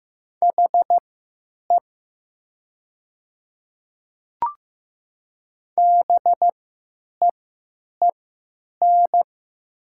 Morse code, heard as a keyed tone, about 15 wpm, spaced as for 9 wpm, 700 Hz. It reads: HE BEEN
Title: 3x Character Spacing - Top 100 Words - 15wpm